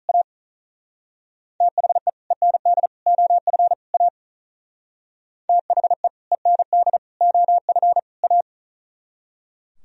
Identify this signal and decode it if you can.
Morse code, heard as a keyed tone, about 38 wpm, 700 Hz. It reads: A THE END OF A THE END OF A
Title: Speed-Racing - Sets of 4 Words 35wpm